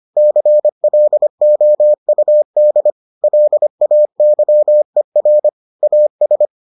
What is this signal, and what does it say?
Morse code, heard as a keyed tone, about 25 wpm, 600 Hz. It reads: CLOUD LAYER AS